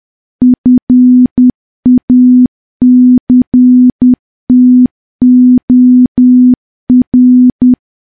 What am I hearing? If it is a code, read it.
Morse code, heard as a keyed tone, about 10 wpm, 250 Hz. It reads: FACTOR
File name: morse / morse-training-10WPM